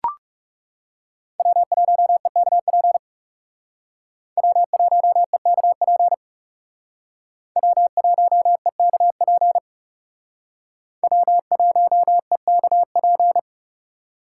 Morse code, heard as a keyed tone, about 36 wpm, 700 Hz. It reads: W1EKP W1EKP W1EKP W1EKP
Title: Speed Racing - USA 1x3 Call Signs 30wpm